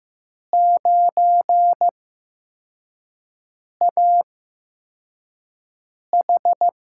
Morse code, heard as a keyed tone, about 15 wpm, spaced as for 4 wpm, 700 Hz. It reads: 9AH
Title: USA Call Signs - 8x Character Spacing - 15wpm